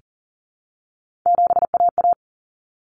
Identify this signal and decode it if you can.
Morse code, heard as a keyed tone, about 40 wpm, 700 Hz. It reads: 7AA
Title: Common Cut Numbers - 3-Digit 40wpm